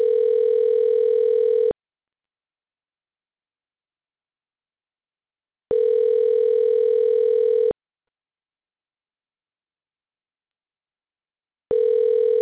Ringback tone (North America), 440 + 480 Hz.